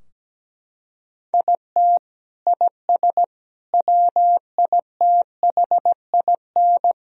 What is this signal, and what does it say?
Morse code, heard as a keyed tone, about 17 wpm, 700 Hz. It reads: IT IS WITHIN
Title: Sentences from Top 1000 Words - Review Part 4 - 17wpm